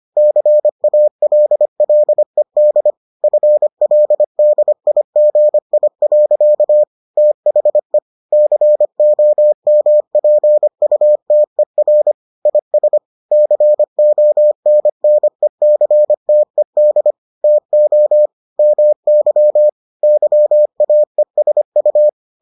Morse code, heard as a keyed tone, about 25 wpm, 600 Hz. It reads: CALLED FLDIGI. THE COMPUTER IS CONNECTED TO MY YAESU